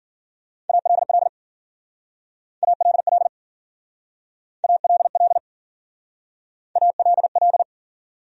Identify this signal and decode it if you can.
Morse code, heard as a keyed tone, about 52 wpm, 700 Hz. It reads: ALL ALL ALL ALL